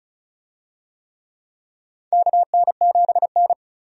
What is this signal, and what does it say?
Morse code, heard as a keyed tone, about 36 wpm, 700 Hz. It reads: KN7N
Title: USA Call Signs - 4 Characters - 35wpm